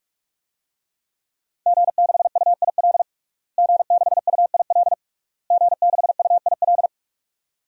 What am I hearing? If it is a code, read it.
Morse code, heard as a keyed tone, about 46 wpm, 700 Hz. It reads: G6UIL G6UIL G6UIL
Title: DXCC-100 Call Signs - 5 Characters - Repeated 3x - 45wpm